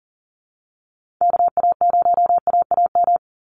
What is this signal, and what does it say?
Morse code, heard as a keyed tone, about 40 wpm, 700 Hz. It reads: KA0AAM
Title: Call Signs Repeated 3x - 40wpm